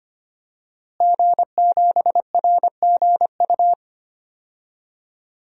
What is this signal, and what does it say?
Morse code, heard as a keyed tone, about 25 wpm, 700 Hz. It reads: G7RGU
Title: DXCC-100 Call Signs - 5 Characters - Repeated 3x - 25wpm